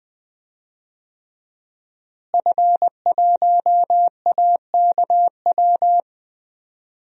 Morse code, heard as a keyed tone, about 20 wpm, 700 Hz. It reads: F1AKW
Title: DXCC-100 Call Signs - 5 Characters - Repeated 3x - 20wpm